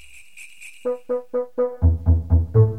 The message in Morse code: none